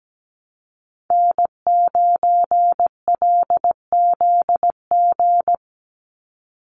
Morse code, heard as a keyed tone, about 17 wpm, 700 Hz. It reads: N9LZG
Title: USA Call Signs 17wpm